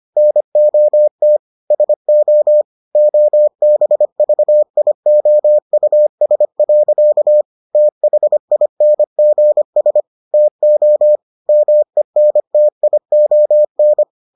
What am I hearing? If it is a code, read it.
Morse code, heard as a keyed tone, about 25 wpm, 600 Hz. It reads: NOT SO OBVIOUS. THINGS TO MENTION